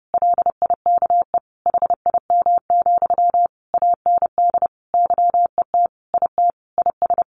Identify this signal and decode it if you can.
Morse code, heard as a keyed tone, about 30 wpm, 700 Hz. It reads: LIKE HIM, AND YET IT IS
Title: Sentences from Top 800 Words 30wpm